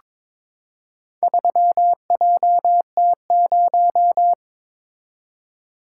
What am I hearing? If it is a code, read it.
Morse code, heard as a keyed tone, about 22 wpm, 700 Hz. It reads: 3JT0
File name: practice / 22wpm / Four Letter-Number Combinations - 22wpm